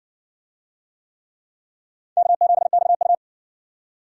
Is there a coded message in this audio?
Yes: Morse code from 2.2 to 3.1 s, heard as a keyed tone, about 56 wpm, 700 Hz. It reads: K7XU